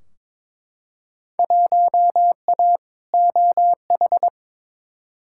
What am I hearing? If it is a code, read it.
Morse code, heard as a keyed tone, about 22 wpm, 700 Hz. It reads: 1A OH
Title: Field Day Exchanges - 22wpm